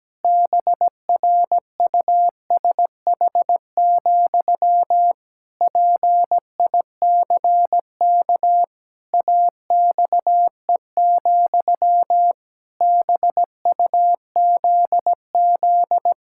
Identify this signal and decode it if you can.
Morse code, heard as a keyed tone, about 17 wpm, 700 Hz. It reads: BRUSH, PICK AXE, BUZZ
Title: Pangram Challenge v2 - 17wpm